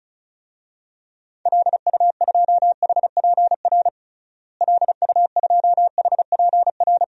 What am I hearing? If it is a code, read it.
Morse code, heard as a keyed tone, about 34 wpm, 700 Hz. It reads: LU2HPR LU2HPR
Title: DXCC-100 Call Signs - 6 Characters - Repeated 2x - 35wpm